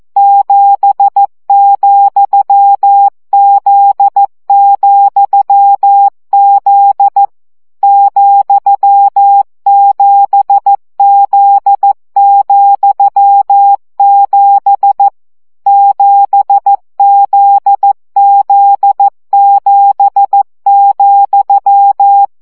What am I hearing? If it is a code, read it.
Morse code, heard as a keyed tone, about 14 wpm, 800 Hz. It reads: 7,Z,Z ,7Z,7 7ZZ7,
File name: CW gyakorlas z 7 vesszo 12 baud